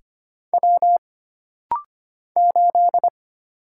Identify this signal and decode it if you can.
Morse code, heard as a keyed tone, about 25 wpm, 700 Hz. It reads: W 8
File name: Single Character (Letter:Number:Punctuation:Prosign) - Rapid-Fire - 25wpm